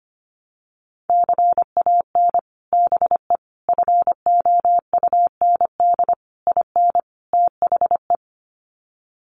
Morse code, heard as a keyed tone, about 25 wpm, 700 Hz. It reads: CAN BE FOUND IN THE
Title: Sets of 5 Words Encore 25wpm